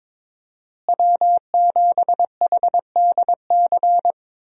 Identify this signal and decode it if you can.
Morse code, heard as a keyed tone, about 21 wpm, 700 Hz. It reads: W7HDC
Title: USA Call Signs - 5 Characters - Repeated 2x - 22wpm